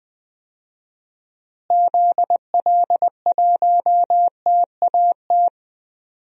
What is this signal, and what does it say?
Morse code, heard as a keyed tone, about 20 wpm, 700 Hz. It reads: ZL1TAT
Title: DXCC-100 Call Signs - 6 Characters - Repeated 3x - 20wpm